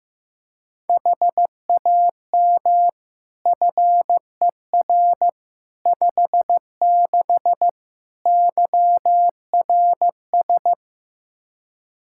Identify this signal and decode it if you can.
Morse code, heard as a keyed tone, about 15 wpm, 700 Hz. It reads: HAM FER 56 YRS
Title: K1USN SST QSO Practice 15wpm